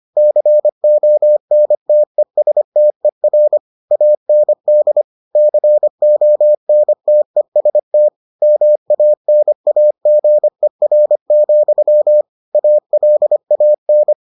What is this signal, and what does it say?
Morse code, heard as a keyed tone, about 25 wpm, 600 Hz. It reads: CONTESTER AND CONTEST MANAGER, ALAN